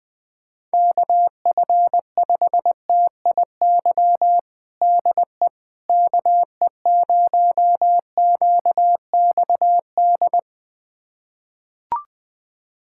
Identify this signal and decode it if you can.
Morse code, heard as a keyed tone, about 20 wpm, 700 Hz. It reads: KF5TIY DE KE0QXD
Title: USA Callsign DE Callsign - 6 Characters - 20wpm